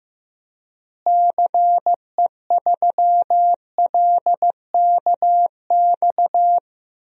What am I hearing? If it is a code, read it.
Morse code, heard as a keyed tone, about 15 wpm, 700 Hz. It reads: CE3LKX
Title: DXCC-100 Call Signs - 6 Characters - Repeated 2x - 15wpm